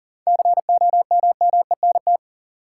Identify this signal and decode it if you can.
Morse code, heard as a keyed tone, about 40 wpm, 700 Hz. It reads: COMMENT
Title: Seven Letter Words - 40wpm